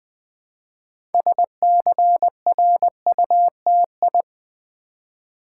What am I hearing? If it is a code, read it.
Morse code, heard as a keyed tone, about 20 wpm, 700 Hz. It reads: SCRUTI
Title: Build A Word - 2 Letters At A Time - 8-Letter Words - 20wpm